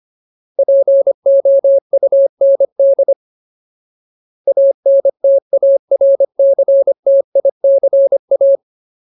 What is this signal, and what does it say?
Morse code, heard as a keyed tone, about 25 wpm, 550 Hz. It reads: POUND ANTARCTICA
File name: Ultimate Word List / 25WPM Ultimate